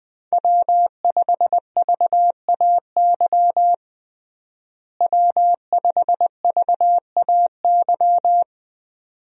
Morse code, heard as a keyed tone, about 20 wpm, 700 Hz. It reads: W5VAY W5VAY
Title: USA Call Signs - 5 Characters - Repeated 2x - 20wpm